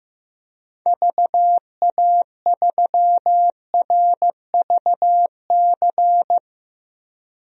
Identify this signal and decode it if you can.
Morse code, heard as a keyed tone, about 15 wpm, 700 Hz. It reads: VA3RVC
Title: DXCC-100 Call Signs - 6 Characters - 15wpm